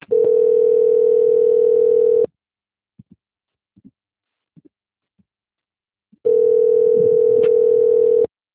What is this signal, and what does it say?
Ringback tone (North America), 440 + 480 Hz.